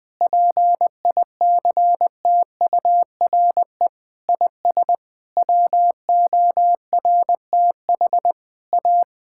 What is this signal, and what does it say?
Morse code, heard as a keyed tone, about 20 wpm, 700 Hz. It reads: PICTURE IS WORTH A
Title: Sentences with 7-Word Phrases - 20wpm